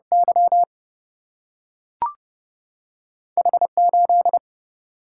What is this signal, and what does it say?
Morse code, heard as a keyed tone, about 30 wpm, 700 Hz. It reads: Y H8
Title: Single-Letter Single-Number 30wpm